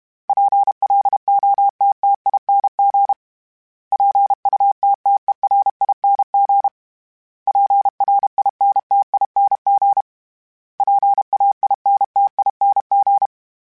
Morse code, heard as a keyed tone, about 32 wpm, 800 Hz. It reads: PLOTTING PUTTERING PRINTING PAINTING
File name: Special INGs-mp3s / Special INGs-mp3s / Sngl Ing @30